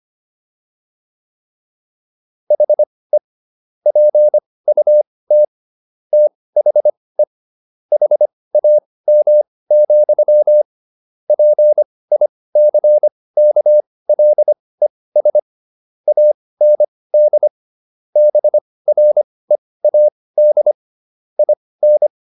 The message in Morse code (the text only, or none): HE PUT THE HAM, PICKLES AND BREAD IN